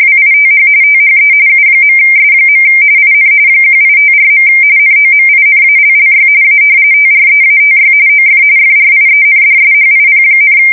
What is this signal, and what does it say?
RTTY, about 45.45 baud. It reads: WITH THE U.S. IN 1854. NOMINALLY A TREATY OF FRIENDSHIP,
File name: fsk170 / usnavy2